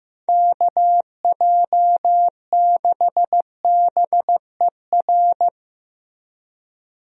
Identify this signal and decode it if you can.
Morse code, heard as a keyed tone, about 15 wpm, 700 Hz. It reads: KJ6BER
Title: USA Call Signs - 6 Characters - Repeated 3x - 15wpm